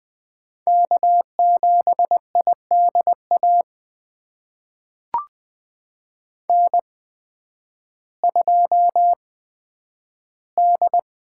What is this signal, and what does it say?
Morse code, heard as a keyed tone, about 20 wpm, spaced as for 11 wpm, 700 Hz. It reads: K7IDA N2D
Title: USA Call Signs - 5 Characters - 8x Character Spacing - 20wpm